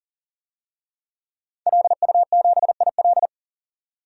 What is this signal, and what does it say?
Morse code, heard as a keyed tone, about 40 wpm, 700 Hz. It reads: LU7IL
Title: DXCC-100 Call Signs - 5 Characters - 40wpm